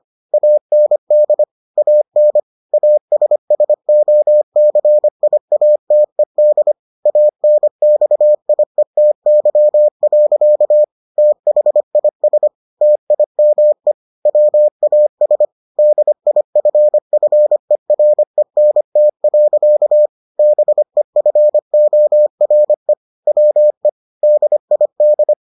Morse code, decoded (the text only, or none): AND AN ASSOCIATED ANXIETY. THIS TIME WAS DIFFERENT. BEFORE WE DID